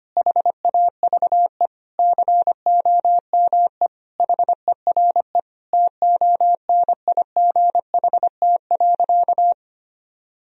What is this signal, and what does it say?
Morse code, heard as a keyed tone, about 25 wpm, 700 Hz. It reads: HAVE COME HERE TONIGHT.
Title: Sentences from Top 1000 Words - Review Part 3 - 25wpm